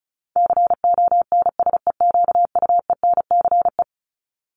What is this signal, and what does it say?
Morse code, heard as a keyed tone, about 34 wpm, 700 Hz. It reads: CONSEQUENCE
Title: LY Suffix - Top 100 Words with Base Words 35wpm